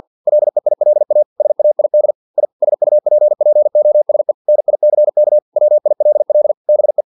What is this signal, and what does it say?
Morse code, heard as a keyed tone, about 50 wpm, 600 Hz. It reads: LEILA SAID I SUPPOSE NICK WILL BE
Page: Never 49wpm